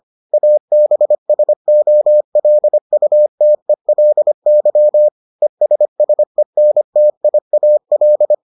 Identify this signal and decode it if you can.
Morse code, heard as a keyed tone, about 25 wpm, 600 Hz. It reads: ABSOLUTELY ESSENTIAL